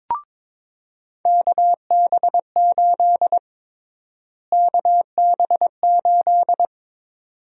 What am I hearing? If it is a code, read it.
Morse code, heard as a keyed tone, about 22 wpm, 700 Hz. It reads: KB8 KB8
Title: USA Call Sign Prefixes - Repeated 2x - 22wpm